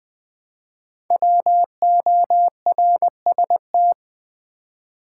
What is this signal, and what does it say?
Morse code, heard as a keyed tone, about 20 wpm, 700 Hz. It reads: WORST
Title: Build A Word - 1 Letter At A Time - 3 to 5-Letter Words - 20wpm